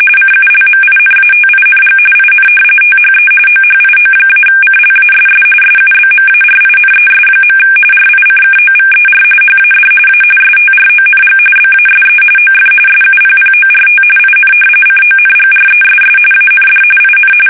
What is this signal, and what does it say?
RTTY, about 45.45 baud. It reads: CULMINATING IN THE OHIO-CLASS SUBMARINES. RICKOVER HAD A STRONG BASE OF SUPPORT IN CONGRESS AND IN
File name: fsk850 / usnavy5